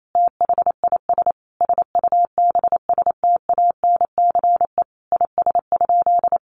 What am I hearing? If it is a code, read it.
Morse code, heard as a keyed tone, about 28 wpm, 700 Hz. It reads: THIS SUBSTANCE IS?
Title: Sentences from Top 1500 Words 28wpm